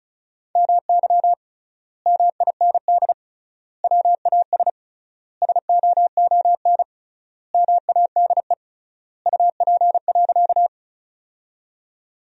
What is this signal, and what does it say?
Morse code, heard as a keyed tone, about 36 wpm, 700 Hz. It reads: MY MIND WAS SOON MADE UP.
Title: 3x Word-Spacing - Sentences from Top 500 Words - 35wpm